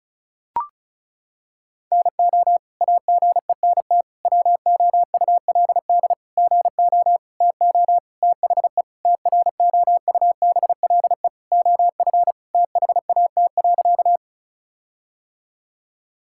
Morse code, heard as a keyed tone, about 36 wpm, 700 Hz. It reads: NO AGENT WOULD GO TO THE TROUBLE OF THAT.
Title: Sentences from Top 1600 Words 35wpm